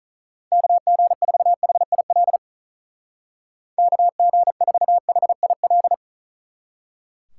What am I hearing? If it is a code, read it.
Morse code, heard as a keyed tone, about 44 wpm, 700 Hz. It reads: KG4HIL KG4HIL
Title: Speed Racing - USA Call Signs 35wpm